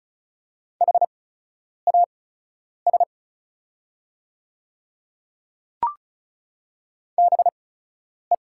Morse code, heard as a keyed tone, about 36 wpm, spaced as for 10 wpm, 700 Hz. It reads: HAS BE
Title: 8x Character Spacing - Top 100 Words - 35wpm